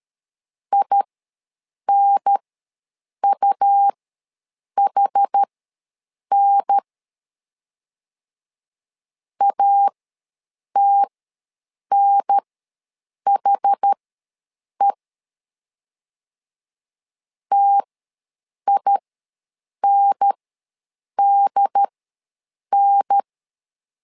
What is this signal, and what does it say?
Morse code, heard as a keyed tone, about 13 wpm, spaced as for 7 wpm, 800 Hz. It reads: INUHN ATNHE TINDN
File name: LEZIONI / 2 UHDN / 2 UHDN ws 12wpm 1708